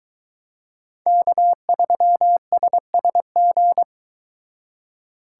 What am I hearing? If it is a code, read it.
Morse code, heard as a keyed tone, about 23 wpm, 700 Hz. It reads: K3SSG